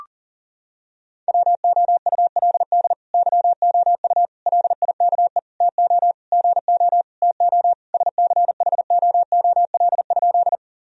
Morse code, heard as a keyed tone, about 40 wpm, 700 Hz. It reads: WOULD YOU LIKE TO GO TO SCHOOL?
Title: Sentences from Top 1000 Words - Review Part 1 - 40wpm